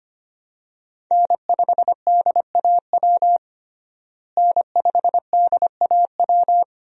Morse code, heard as a keyed tone, about 25 wpm, 700 Hz. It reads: N5DAW N5DAW
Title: USA Call Signs - 5 Characters - Repeated 3x - 25wpm